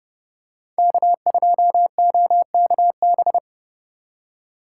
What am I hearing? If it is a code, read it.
Morse code, heard as a keyed tone, about 29 wpm, 700 Hz. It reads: K2OKB